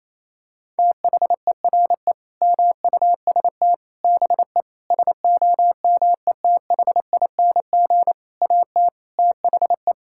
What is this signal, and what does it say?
Morse code, heard as a keyed tone, about 28 wpm, 700 Hz. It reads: THERE MUST BE SOMETHING AT THE
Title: Sentences from Top 1000 Words - Review Part 6 - 28wpm